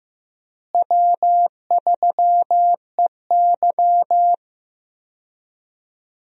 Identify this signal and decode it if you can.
Morse code, heard as a keyed tone, about 15 wpm, 700 Hz. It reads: W3EY
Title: USA Call Signs - 4 Characters - Repeated 2x - 15wpm